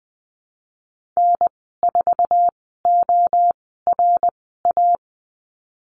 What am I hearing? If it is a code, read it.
Morse code, heard as a keyed tone, about 19 wpm, spaced as for 14 wpm, 700 Hz. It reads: N4ORA
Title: 055 Lesson 10g - Letter r - Call Signs 20wpm